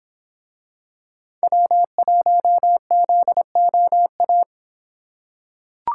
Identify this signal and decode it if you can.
Morse code, heard as a keyed tone, about 26 wpm, 700 Hz. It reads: W1ZOA